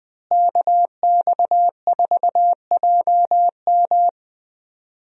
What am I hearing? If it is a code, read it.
Morse code, heard as a keyed tone, about 20 wpm, 700 Hz. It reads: KX4JM
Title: USA Call Signs - 5 Characters - Repeated 3x - 20wpm